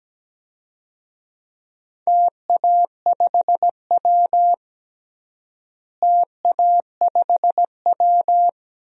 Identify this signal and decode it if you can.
Morse code, heard as a keyed tone, about 17 wpm, 700 Hz. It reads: TA5W TA5W
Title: DXCC-100 Call Signs - 4 Characters - Repeated 2x - 17wpm